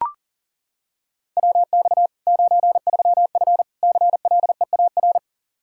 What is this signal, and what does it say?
Morse code, heard as a keyed tone, about 40 wpm, 700 Hz. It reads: WX 93F CLEAR
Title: Practice QSO - Exchange 2 - Info - 40wpm